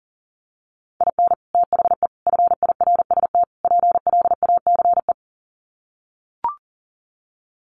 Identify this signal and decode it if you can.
Morse code, heard as a keyed tone, about 40 wpm, 700 Hz. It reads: IN THE FIRST PLACE
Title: Sets of 4 Words 40wpm